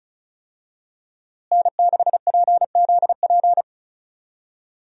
Morse code, heard as a keyed tone, about 36 wpm, 700 Hz. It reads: N6PZP
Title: USA Call Signs - 5 Characters - Repeated 2x - 35wpm